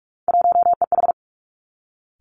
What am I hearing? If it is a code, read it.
Morse code, heard as a keyed tone, about 46 wpm, 700 Hz. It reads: 1EH
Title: Three Challenging Letter-Number Combinations 45wpm